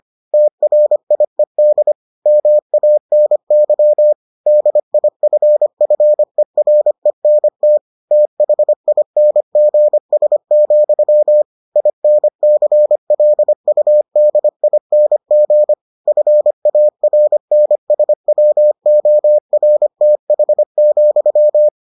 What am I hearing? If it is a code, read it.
Morse code, heard as a keyed tone, about 25 wpm, 600 Hz. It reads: TRIED MANY DIFFERENT THINGS, INCLUDING FARNSWORTH,